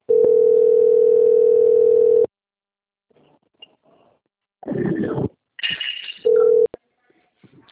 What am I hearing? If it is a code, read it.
Ringback tone (North America), 440 + 480 Hz.